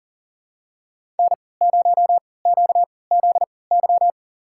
Morse code, heard as a keyed tone, about 40 wpm, spaced as for 23 wpm, 700 Hz. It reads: N0QZY
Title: USA Call Signs - 5 Characters - 3x Character Spacing - 40wpm